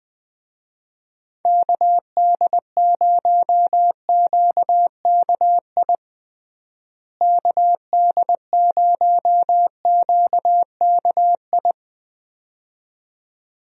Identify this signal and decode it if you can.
Morse code, heard as a keyed tone, about 20 wpm, 700 Hz. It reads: KD0QKI KD0QKI
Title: USA Call Signs - 6 Characters - Repeated 2x - 20wpm